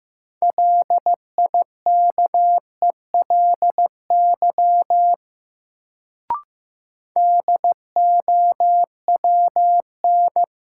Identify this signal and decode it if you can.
Morse code, heard as a keyed tone, about 15 wpm, 700 Hz. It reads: LIKELY DOWN
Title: Sets of 2 Words - New Word Drill - 15wpm